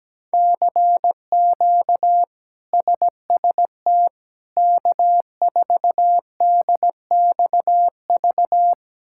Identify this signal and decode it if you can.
Morse code, heard as a keyed tone, about 17 wpm, 700 Hz. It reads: CQ SST K4DXV
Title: K1USN SST QSO Practice - 17wpm